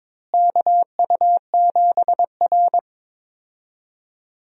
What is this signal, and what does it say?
Morse code, heard as a keyed tone, about 22 wpm, 700 Hz. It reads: KU7R
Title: USA Call Signs - 4 Characters - Repeated 2x - 22wpm